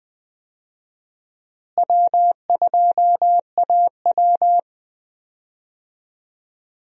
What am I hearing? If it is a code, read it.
Morse code, heard as a keyed tone, about 21 wpm, 700 Hz. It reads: W2AW
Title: USA Call Signs - 4 Characters - Repeated 3x - 20wpm